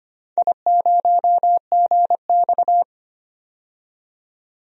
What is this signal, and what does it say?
Morse code, heard as a keyed tone, about 25 wpm, 700 Hz. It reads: I0GX
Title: DXCC-100 Call Signs - 4 Characters - 25wpm